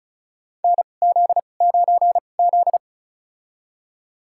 Morse code, heard as a keyed tone, about 34 wpm, spaced as for 25 wpm, 690 Hz. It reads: NZ9Z
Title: USA Call Signs - 4 Characters - 2x Character Spacing - 35wpm